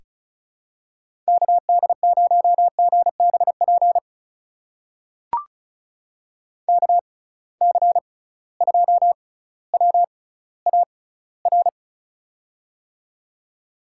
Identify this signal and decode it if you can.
Morse code, heard as a keyed tone, about 36 wpm, spaced as for 18 wpm, 700 Hz. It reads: KD0GBP KC2WAR
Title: USA Call Signs - 6 Characters - 6x Character Spacing - 35wpm